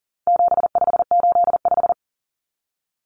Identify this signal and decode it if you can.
Morse code, heard as a keyed tone, about 40 wpm, 700 Hz. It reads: 7585
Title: Common Cut Numbers - 4-Digit 40wpm